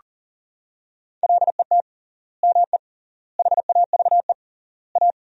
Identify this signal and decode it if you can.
Morse code, heard as a keyed tone, about 40 wpm, 700 Hz. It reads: LET ME HAVE A
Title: 3x Word-Spacing - Sentences from Top 300 Words - 40wpm